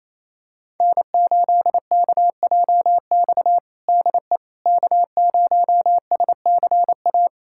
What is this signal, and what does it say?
Morse code, heard as a keyed tone, about 27 wpm, 700 Hz. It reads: N8KJX DE K0SCA
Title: USA Callsign DE Callsign - 5 Characters - 28wpm